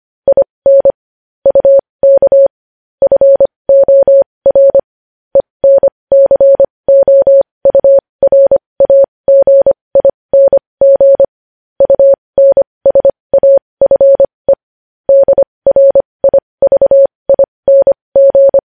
Morse code, heard as a keyed tone, about 25 wpm, spaced as for 20 wpm, 550 Hz. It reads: IN UK FOR ENCOURAGING UNSAFE DRIVING
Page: news-headlines-in-morse-code-at-20-wpm